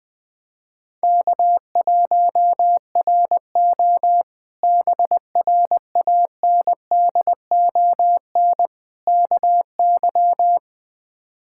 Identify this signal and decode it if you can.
Morse code, heard as a keyed tone, about 21 wpm, 700 Hz. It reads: K1RO BRANDON KY
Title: K1USN SST QSO Practice - 20wpm